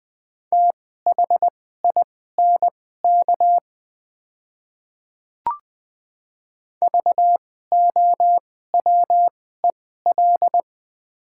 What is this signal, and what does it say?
Morse code, heard as a keyed tone, about 20 wpm, spaced as for 15 wpm, 700 Hz. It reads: THINK VOWEL
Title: 2x Character Spacing - Top 500 Words - 20wpm